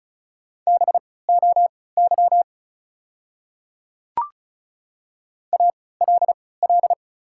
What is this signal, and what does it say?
Morse code, heard as a keyed tone, about 36 wpm, spaced as for 20 wpm, 690 Hz. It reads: BOY ALL
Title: 3x Character Spacing - Top 300 Words - 35wpm